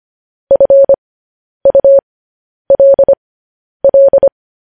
Morse code, heard as a keyed tone, about 25 wpm, spaced as for 10 wpm, 550 Hz. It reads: FULL
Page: news-headlines-in-morse-code-at-10-wpm